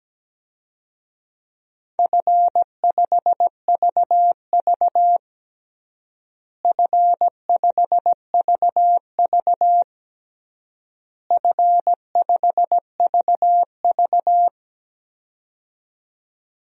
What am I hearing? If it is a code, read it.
Morse code, heard as a keyed tone, about 17 wpm, 700 Hz. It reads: F5VV F5VV F5VV
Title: DXCC-100 Call Signs - 4 Characters - Repeated 3x - 17wpm